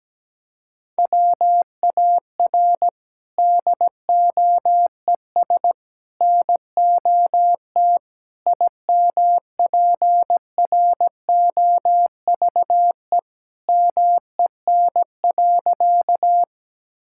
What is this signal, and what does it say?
Morse code, heard as a keyed tone, about 17 wpm, 700 Hz. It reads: WAR DOES NOT IMPROVE MEN.
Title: Sentences from Top 1000 Words - Review Part 5 - 17wpm